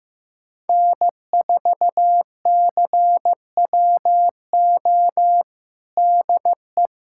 Morse code, heard as a keyed tone, about 16 wpm, 700 Hz. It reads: N4CWO DE
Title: USA Callsign DE Callsign - 5 Characters - 15wpm